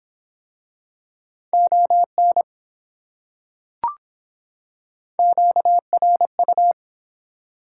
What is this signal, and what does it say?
Morse code, heard as a keyed tone, about 23 wpm, 700 Hz. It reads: ON QRU